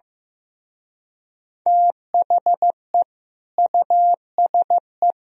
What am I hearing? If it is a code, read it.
Morse code, heard as a keyed tone, about 16 wpm, 700 Hz. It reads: THE USE
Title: Speed-Racing - Sets of 3 Words 15wpm